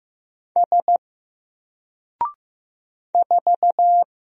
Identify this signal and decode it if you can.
Morse code, heard as a keyed tone, about 15 wpm, 700 Hz. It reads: S 4
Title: Single Character (Letter:Number:Punctuation:Prosign) Alt 15wpm